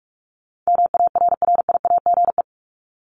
Morse code, heard as a keyed tone, about 44 wpm, 700 Hz. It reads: MARRIAGE
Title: Top 751-1000 Words 45wpm